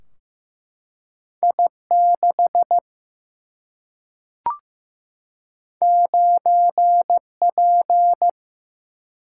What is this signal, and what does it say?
Morse code, heard as a keyed tone, about 15 wpm, 700 Hz. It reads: I 6 9 P
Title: Single-Letter Single-Number 15wpm